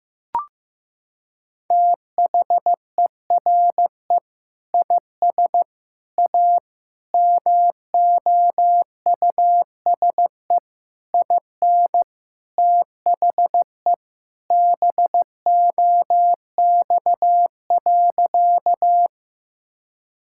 Morse code, heard as a keyed tone, about 15 wpm, 700 Hz. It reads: THERE IS A MOUSE IN THE BOX.
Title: Sentences from Top 1000 Words - Review Part 5 - 15wpm